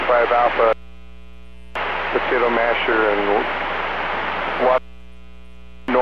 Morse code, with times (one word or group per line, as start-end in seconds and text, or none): none